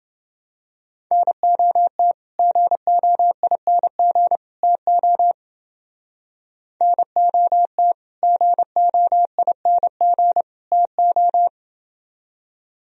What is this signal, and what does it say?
Morse code, heard as a keyed tone, about 29 wpm, 700 Hz. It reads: NOT GOING TO NOT GOING TO